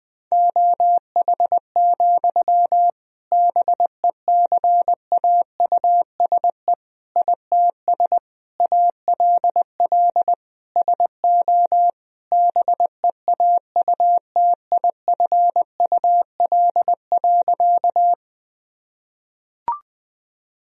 Morse code, heard as a keyed tone, about 20 wpm, 700 Hz. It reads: OH, BECAUSE ITS ALL SO BEAUTIFUL.
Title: Sentences from Top 1000 Words - Review Part 3 - 20wpm